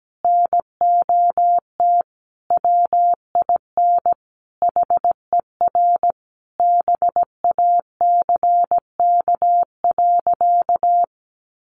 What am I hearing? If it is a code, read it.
Morse code, heard as a keyed tone, about 17 wpm, 700 Hz. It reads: NOT WIN HER BACK.
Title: Sentences from Top 1500 Words 17wpm